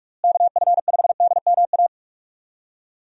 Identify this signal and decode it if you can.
Morse code, heard as a keyed tone, about 46 wpm, 700 Hz. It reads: KF5DMA
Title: USA Call Signs - 6 Characters - Repeated 3x - 45wpm